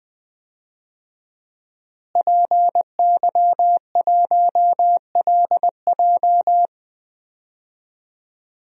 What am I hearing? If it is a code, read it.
Morse code, heard as a keyed tone, about 21 wpm, 700 Hz. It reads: PY1LJ
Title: DXCC-100 Call Signs - 5 Characters - Repeated 2x - 20wpm